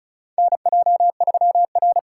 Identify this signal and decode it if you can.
Morse code, heard as a keyed tone, about 36 wpm, 700 Hz. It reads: NJ3R
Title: USA Call Signs - 4 Characters - 8x Character Spacing - 35wpm